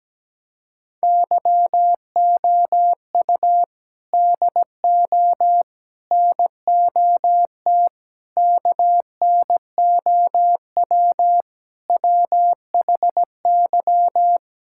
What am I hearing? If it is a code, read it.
Morse code, heard as a keyed tone, about 17 wpm, 700 Hz. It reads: YOU DO NOT KNOW WHY
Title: Sentences from Top 1000 Words - Review Part 6 - 17wpm